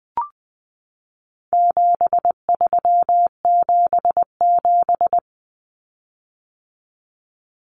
Morse code, heard as a keyed tone, about 20 wpm, 700 Hz. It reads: 7377
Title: Common Cut Numbers - 4-Digit 20wpm